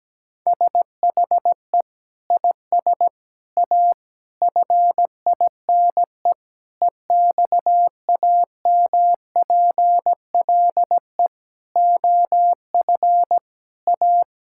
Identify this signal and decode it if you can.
Morse code, heard as a keyed tone, about 17 wpm, 700 Hz. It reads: SHE IS A FINE EXAMPLE OF A
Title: Sentences from Top 1000 Words - Review Part 6 - 17wpm